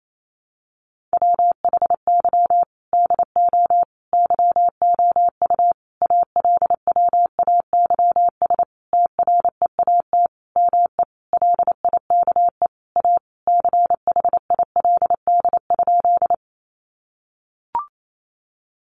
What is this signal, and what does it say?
Morse code, heard as a keyed tone, about 28 wpm, 700 Hz. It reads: WHY DO YOU ALWAYS TREAT ME LIKE A CHILD?
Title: Sentences from Top 1300 Words 28wpm